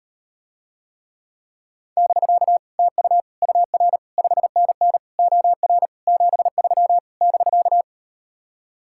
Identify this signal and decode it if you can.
Morse code, heard as a keyed tone, about 39 wpm, 700 Hz. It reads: <BK> TU UR 5NN OR 73 <BK>